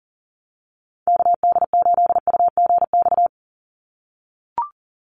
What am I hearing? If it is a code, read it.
Morse code, heard as a keyed tone, about 40 wpm, 700 Hz. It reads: KD8UGX
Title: USA Call Signs 40wpm